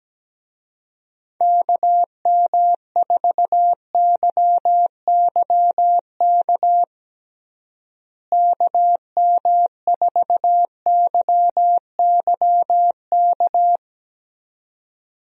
Morse code, heard as a keyed tone, about 17 wpm, 700 Hz. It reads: KM4YYK KM4YYK
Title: USA Call Signs - 6 Characters - Repeated 2x - 17wpm